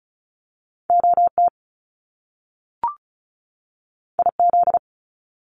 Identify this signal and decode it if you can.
Morse code, heard as a keyed tone, about 36 wpm, 700 Hz. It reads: OT IZ
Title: Top Two Letter Combinations in Words 35wpm